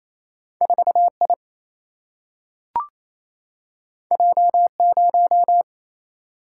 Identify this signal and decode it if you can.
Morse code, heard as a keyed tone, about 28 wpm, 700 Hz. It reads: 4 I J 0
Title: Single-Letter Single-Number 28wpm